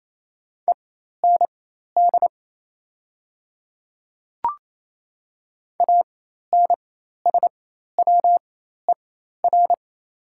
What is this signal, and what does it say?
Morse code, heard as a keyed tone, about 28 wpm, spaced as for 13 wpm, 700 Hz. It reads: END ANSWER
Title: 4x Character Spacing - Top 500 Words - 28wpm